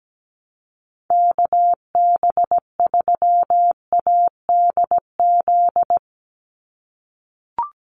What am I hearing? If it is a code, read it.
Morse code, heard as a keyed tone, about 17 wpm, 700 Hz. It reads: KB3ADZ
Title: USA Call Signs 17wpm